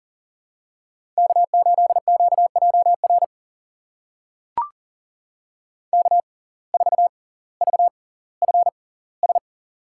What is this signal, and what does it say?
Morse code, heard as a keyed tone, about 40 wpm, spaced as for 20 wpm, 700 Hz. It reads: K8QJR K4VFS
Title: USA Call Signs - 5 Characters - 6x Character Spacing - 40wpm